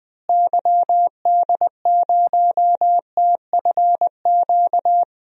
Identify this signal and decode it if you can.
Morse code, heard as a keyed tone, about 20 wpm, 700 Hz. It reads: YD0TFQ
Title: DXCC-100 Call Signs - 6 Characters - Repeated 3x - 20wpm